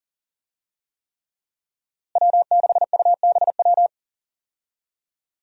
Morse code, heard as a keyed tone, about 40 wpm, 700 Hz. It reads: W6UBW
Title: USA Call Signs - 5 Characters - Repeated 3x - 40wpm